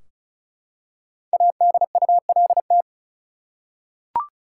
Morse code, heard as a keyed tone, about 34 wpm, 700 Hz. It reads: ADULT
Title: Five Letter Words - 35wpm